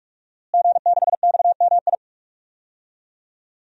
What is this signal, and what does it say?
Morse code, heard as a keyed tone, about 46 wpm, 700 Hz. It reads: G6XMI
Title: DXCC-100 Call Signs - 5 Characters - Repeated 2x - 45wpm